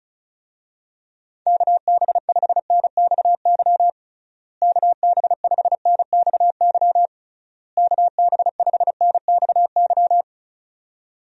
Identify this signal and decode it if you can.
Morse code, heard as a keyed tone, about 34 wpm, 700 Hz. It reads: KB5NXY KB5NXY KB5NXY
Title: USA Call Signs - 6 Characters - Repeated 3x - 35wpm